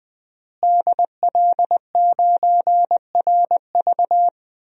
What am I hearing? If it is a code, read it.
Morse code, heard as a keyed tone, about 20 wpm, 700 Hz. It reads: DL9RV
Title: DXCC-100 Call Signs - 5 Characters - Repeated 2x - 20wpm